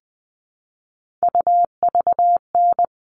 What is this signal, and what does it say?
Morse code, heard as a keyed tone, about 20 wpm, 700 Hz. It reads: UVN
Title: All Cut Numbers - 3-Digit 20wpm